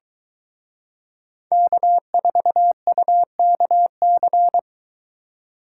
Morse code, heard as a keyed tone, about 22 wpm, 700 Hz. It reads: K4UKC